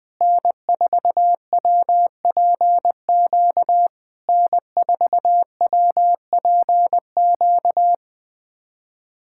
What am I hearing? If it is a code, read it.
Morse code, heard as a keyed tone, about 20 wpm, 700 Hz. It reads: N4WPQ N4WPQ
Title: SOTA QSO Practice 20wpm